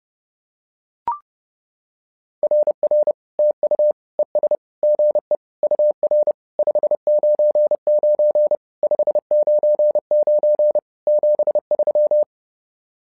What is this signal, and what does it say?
Morse code, heard as a keyed tone, about 31 wpm, 600 Hz. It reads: RR TU ES GE UR 599 599 73
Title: SOTA QSO Practice 30wpm